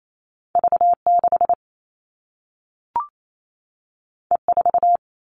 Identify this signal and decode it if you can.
Morse code, heard as a keyed tone, about 28 wpm, 700 Hz. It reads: V6 E4
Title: All Cut Numbers - 2-Digit 28wpm